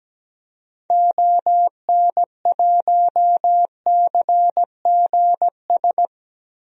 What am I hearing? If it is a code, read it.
Morse code, heard as a keyed tone, about 17 wpm, 700 Hz. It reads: ON1CGS
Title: DXCC-100 Call Signs - 6 Characters - 17wpm